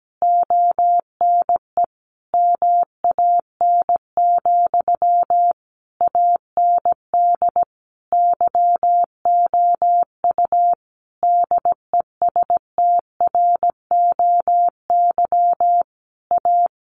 Morse code, heard as a keyed tone, about 17 wpm, 700 Hz. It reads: ONE MAN, AND YOU DESTROY A
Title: A Sound Of Thunder 17wpm